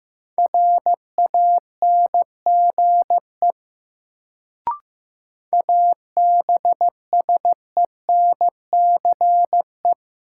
Morse code, heard as a keyed tone, about 15 wpm, 700 Hz. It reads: RANGE ABSENCE
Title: Sets of 4 Words - New Word Drill - 15wpm